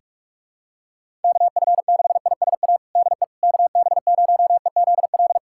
Morse code, heard as a keyed tone, about 46 wpm, 700 Hz. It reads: KF6ISA DE KB0EZL
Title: USA Callsign DE Callsign - 6 Characters - 45wpm